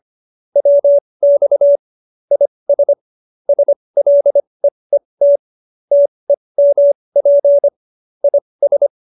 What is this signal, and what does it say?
Morse code, heard as a keyed tone, about 25 wpm, spaced as for 20 wpm, 570 Hz. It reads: WX IS SLEET TEMP IS
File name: QSO307 20